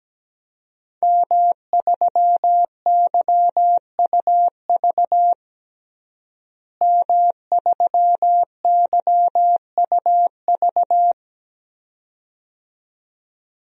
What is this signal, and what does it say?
Morse code, heard as a keyed tone, about 17 wpm, 700 Hz. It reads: M3YUV M3YUV
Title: DXCC-100 Call Signs - 5 Characters - Repeated 2x - 17wpm